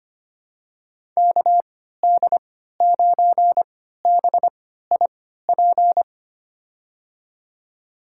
Morse code, heard as a keyed tone, about 25 wpm, spaced as for 14 wpm, 700 Hz. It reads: KD9BIP
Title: USA Call Signs - 6 Characters - 3x Character Spacing - 25wpm